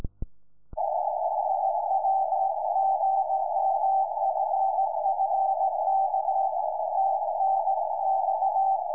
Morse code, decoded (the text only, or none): none